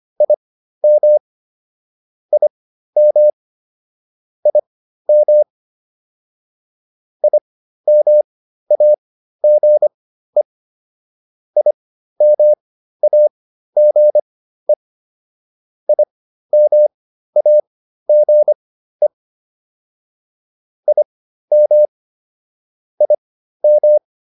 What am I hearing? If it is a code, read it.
Morse code, heard as a keyed tone, about 25 wpm, spaced as for 13 wpm, 600 Hz. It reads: IM IM IM IMAGE IMAGE IMAGE IM IM